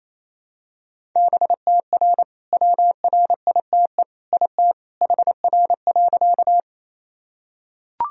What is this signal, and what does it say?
Morse code, heard as a keyed tone, about 27 wpm, 700 Hz. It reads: BTR WRITE IT HR.
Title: Sentences from Top 400 Words and Abbreviations 28wpm